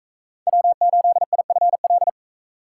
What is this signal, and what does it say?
Morse code, heard as a keyed tone, about 41 wpm, 700 Hz. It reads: W8IFL